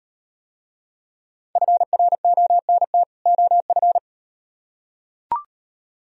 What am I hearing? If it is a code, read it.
Morse code, heard as a keyed tone, about 35 wpm, 700 Hz. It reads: FRONT OF